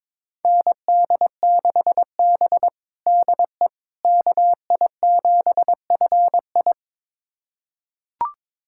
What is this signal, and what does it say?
Morse code, heard as a keyed tone, about 22 wpm, 700 Hz. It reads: ND6B DE KI7FI
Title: USA Callsign DE Callsign 22wpm